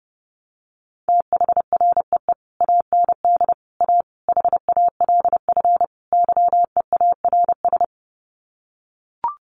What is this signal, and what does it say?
Morse code, heard as a keyed tone, about 30 wpm, 700 Hz. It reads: THREE AND A HALF YEARS
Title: Sets of 5 Words Encore2 30wpm